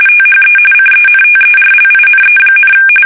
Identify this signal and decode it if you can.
RTTY, about 45.45 baud. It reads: THE CENTERLINE,
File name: fsk850 / USS-Massachusetts